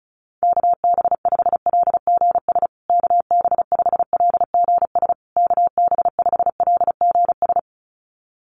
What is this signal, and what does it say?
Morse code, heard as a keyed tone, about 34 wpm, 700 Hz. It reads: KB5LGS KB5LGS KB5LGS
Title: USA Call Signs Repeated 3x 35wpm